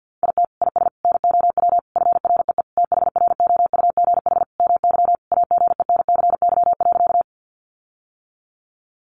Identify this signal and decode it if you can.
Morse code, heard as a keyed tone, about 50 wpm, 700 Hz. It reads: IT IS NOW FREE THROUGH MY AGENCY.
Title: Sentences from Top 1300 Words 50wpm